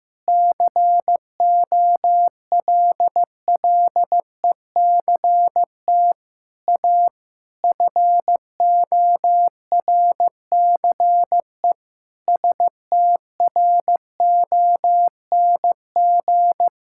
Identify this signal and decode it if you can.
Morse code, heard as a keyed tone, about 15 wpm, 700 Hz. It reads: COLLECT A FORCE STRONG
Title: Sentences from Top 1600 Words 15wpm